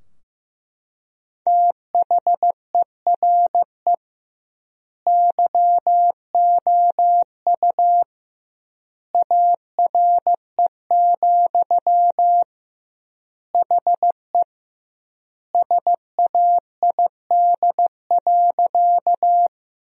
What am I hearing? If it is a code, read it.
Morse code, heard as a keyed tone, about 15 wpm, 700 Hz. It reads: THERE YOU ARE, HE SAID.
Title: 2x Word-Spacing - Sentences from Top 100 Words - 15wpm